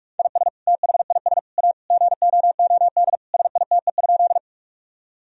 Morse code, heard as a keyed tone, about 41 wpm, 700 Hz. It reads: IS THIS A GOOD SITE?
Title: Sentences from Top 1000 Words - Review Part 4 - 45wpm